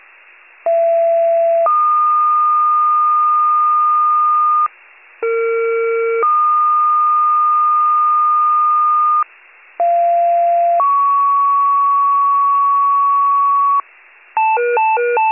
Two-tone page, heard as a one-second tone, then a three-second tone, about 670 then 1220 Hz, about 460 then 1220 Hz, about 690 then 1120 Hz. It is heard from 0.7 to 13.8 s.